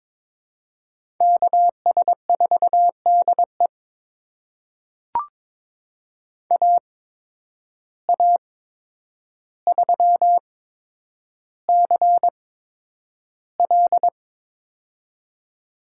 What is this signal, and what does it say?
Morse code, heard as a keyed tone, about 22 wpm, spaced as for 9 wpm, 700 Hz. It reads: KS4DE AA3CL
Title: USA Call Signs - 8x Character Spacing - 22wpm